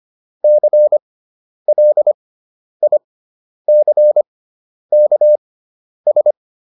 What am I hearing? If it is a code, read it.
Morse code, heard as a keyed tone, about 25 wpm, spaced as for 10 wpm, 600 Hz. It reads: CLICKS